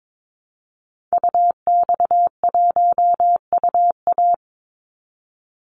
Morse code, heard as a keyed tone, about 22 wpm, 700 Hz. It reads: UX1UA
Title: Call Signs - 22wpm